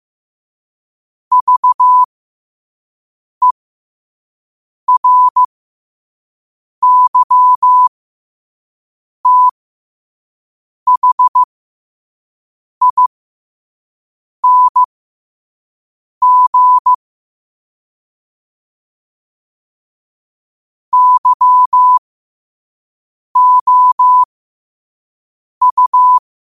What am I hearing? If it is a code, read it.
Morse code, heard as a keyed tone, about 15 wpm, spaced as for 5 wpm, 1000 Hz. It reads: VERYTHING YOU
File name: qotdfiles / 5wpm